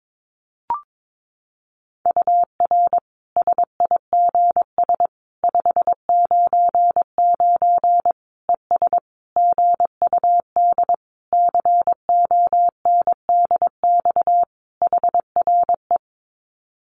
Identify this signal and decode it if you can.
Morse code, heard as a keyed tone, about 22 wpm, 700 Hz. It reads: UR SIGS 599 ES GUD CONDX HRE
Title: Common QSO Phrases 22wpm